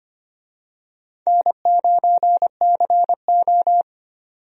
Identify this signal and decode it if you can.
Morse code, heard as a keyed tone, about 25 wpm, 700 Hz. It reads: N9CO
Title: USA Call Signs - 4 Characters - 25wpm